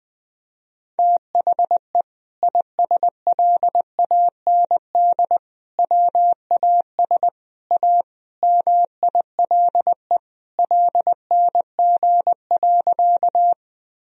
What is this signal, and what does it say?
Morse code, heard as a keyed tone, about 21 wpm, 700 Hz. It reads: THE ISLAND WAS A MILE LNG.
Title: Sentences from Top 500 Words and Abbreviations 20wpm